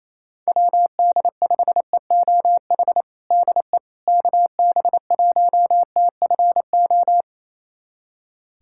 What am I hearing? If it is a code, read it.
Morse code, heard as a keyed tone, about 28 wpm, 700 Hz. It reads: WD5EOH DE KB1TFO
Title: USA Callsign DE Callsign - 6 Characters - 28wpm